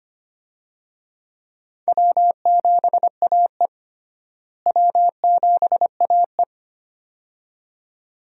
Morse code, heard as a keyed tone, about 25 wpm, 700 Hz. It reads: W7AE W7AE
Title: USA Call Signs - 4 Characters - Repeated 2x - 25wpm